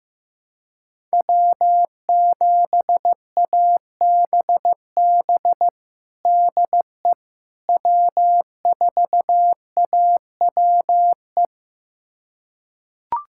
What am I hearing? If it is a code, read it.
Morse code, heard as a keyed tone, about 15 wpm, 700 Hz. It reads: W7ABB DE W4AWE
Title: USA Callsign DE Callsign - 5 Characters - 15wpm